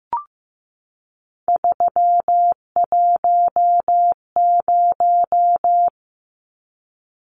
Morse code, heard as a keyed tone, about 15 wpm, 700 Hz. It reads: 310
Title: Numbers - 3-Digit 15wpm